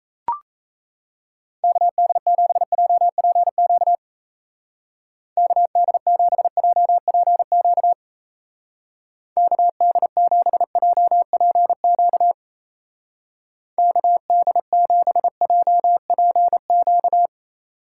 Morse code, heard as a keyed tone, about 34 wpm, 700 Hz. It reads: KD7JPQ KD7JPQ KD7JPQ KD7JPQ
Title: Speed Racing - USA 2x3 Call Signs 28wpm